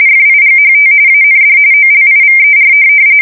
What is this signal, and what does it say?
RTTY, about 45.45 baud. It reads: FIGHT AND OLIVER
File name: fsk170 / USS-New-Jersey